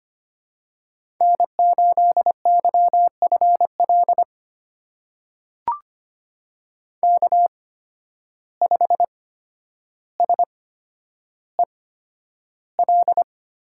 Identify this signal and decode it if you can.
Morse code, heard as a keyed tone, about 25 wpm, spaced as for 11 wpm, 700 Hz. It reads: N8YFL K5SEL
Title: USA Call Signs - 5 Characters - 8x Character Spacing - 25wpm